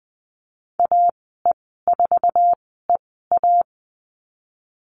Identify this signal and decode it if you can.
Morse code, heard as a keyed tone, about 20 wpm, spaced as for 15 wpm, 700 Hz. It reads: AE4EA
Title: 055 Lesson 10g - Letter r - Call Signs 20wpm